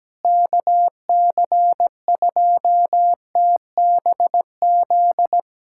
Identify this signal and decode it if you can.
Morse code, heard as a keyed tone, about 17 wpm, 700 Hz. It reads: KC2TBZ
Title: USA Callsign DE Callsign - 6 Characters - 17wpm